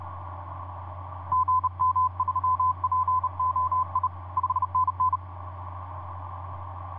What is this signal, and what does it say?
Morse code, heard as a keyed tone, about 31 wpm, 1020 Hz, with radio noise. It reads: GM3POI 5NN
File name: vp8dmn ja1ghh etc 30m cw 23jan2010 2351